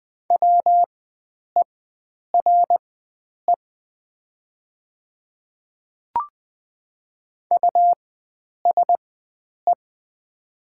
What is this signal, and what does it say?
Morse code, heard as a keyed tone, about 20 wpm, spaced as for 9 wpm, 700 Hz. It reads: WERE USE
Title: 4x Character Spacing - Top 100 Words - 20wpm